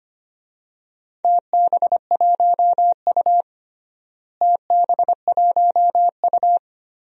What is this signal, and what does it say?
Morse code, heard as a keyed tone, about 25 wpm, 700 Hz. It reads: TB1U TB1U
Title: DXCC-100 Call Signs - 4 Characters - Repeated 2x - 25wpm